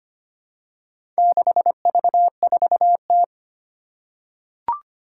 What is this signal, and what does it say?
Morse code, heard as a keyed tone, about 25 wpm, 700 Hz. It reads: 6V4T
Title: Four Challenging Letter-Number Combinations 25wpm